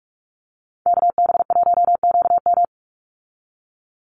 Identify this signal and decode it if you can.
Morse code, heard as a keyed tone, about 44 wpm, 700 Hz. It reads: KB1QM
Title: USA Call Signs 45wpm